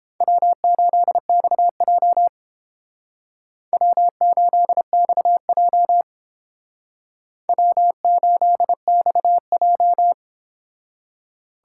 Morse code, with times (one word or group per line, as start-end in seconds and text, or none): 0.2-2.3 W8XJ
3.7-6.0 W8XJ
7.5-10.1 W8XJ